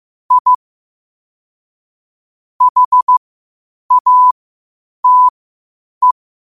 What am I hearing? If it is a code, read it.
Morse code, heard as a keyed tone, about 15 wpm, spaced as for 8 wpm, 1000 Hz. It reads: I HATE
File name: qotdfiles / 8wpm